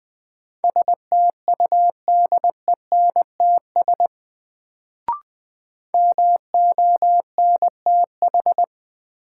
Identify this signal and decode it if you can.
Morse code, heard as a keyed tone, about 20 wpm, 700 Hz. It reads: STUDENTS MONTH
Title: Sets of 5 Words Encore - New Word Drill - 20wpm